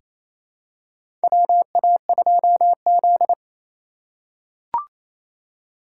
Morse code, heard as a keyed tone, about 28 wpm, 700 Hz. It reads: WA2Z
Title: USA Call Signs - 4 Characters - 4x Character Spacing - 28wpm